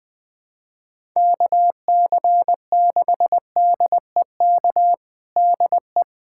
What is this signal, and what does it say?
Morse code, heard as a keyed tone, about 20 wpm, 700 Hz. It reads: KC6DEK DE
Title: USA Callsign DE Callsign - 6 Characters - 20wpm